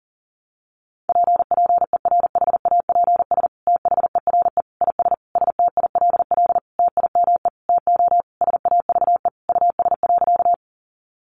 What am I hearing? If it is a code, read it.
Morse code, heard as a keyed tone, about 37 wpm, 700 Hz. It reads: PPERHAPS THERE IS STILL TIME TO SAVE US.
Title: Sentences from Top 1100 Words 40wpm